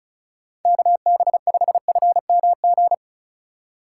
Morse code, heard as a keyed tone, about 36 wpm, 700 Hz. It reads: KB5FMG
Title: USA Call Signs - 6 Characters - 2x Character Spacing - 35wpm